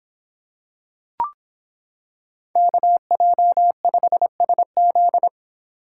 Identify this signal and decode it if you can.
Morse code, heard as a keyed tone, about 26 wpm, 700 Hz. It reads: KJ5SZ